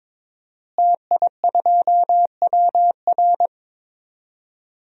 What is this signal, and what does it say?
Morse code, heard as a keyed tone, about 22 wpm, 700 Hz. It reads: TI2WR
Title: DXCC-100 Call Signs - 5 Characters - Repeated 3x - 22wpm